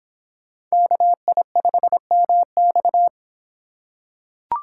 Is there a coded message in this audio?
Yes: Morse code from 0.7 to 3.1 s, heard as a keyed tone, about 26 wpm, 700 Hz. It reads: KI5MX